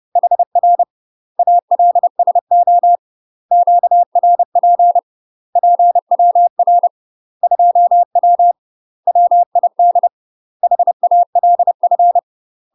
Morse code, heard as a keyed tone, about 30 wpm, 700 Hz. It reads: HR ALSO QRP PWR 2W WID HALF
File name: UK MASTER QSOs TRAINING / 30WPM / UK MASTER QSOs 0036